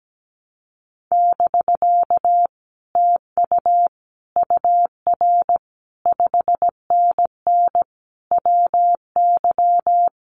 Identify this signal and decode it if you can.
Morse code, heard as a keyed tone, about 17 wpm, 700 Hz. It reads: <BK> TU UR 5NN WY
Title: Common QSO Phrases Encore 17wpm